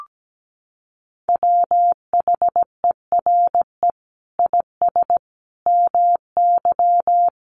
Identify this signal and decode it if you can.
Morse code, heard as a keyed tone, about 17 wpm, 700 Hz. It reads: WHERE IS MY
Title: Sentences from Top 1400 Words 17wpm